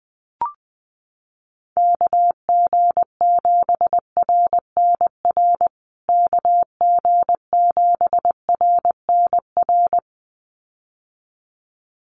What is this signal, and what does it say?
Morse code, heard as a keyed tone, about 20 wpm, 700 Hz. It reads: KG7RNR KG7RNR
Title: USA Call Signs Repeated 2x 20wpm